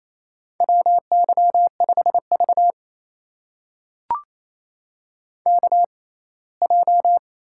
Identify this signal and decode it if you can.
Morse code, heard as a keyed tone, about 28 wpm, spaced as for 19 wpm, 700 Hz. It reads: WY5V KJ
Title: USA Call Signs - 6x Character Spacing - 28wpm